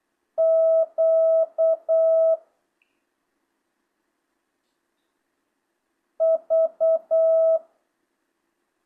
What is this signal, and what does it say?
Morse code, heard as a keyed tone, about 8 wpm, 650 Hz. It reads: Q V
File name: jh8cbh / 2025cw